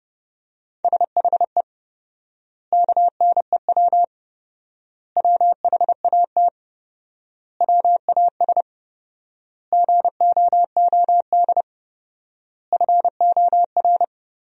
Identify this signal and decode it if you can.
Morse code, heard as a keyed tone, about 30 wpm, 700 Hz. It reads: SHE KNEW WHAT WAS GOOD FOR
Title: 4x Word-Spacing - Sentences from Top 400 Words - 30wpm